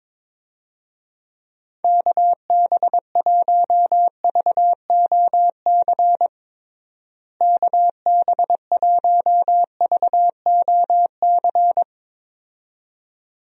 Morse code, heard as a keyed tone, about 21 wpm, 700 Hz. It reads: KB1VOC KB1VOC
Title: USA Call Signs - 6 Characters - Repeated 2x - 22wpm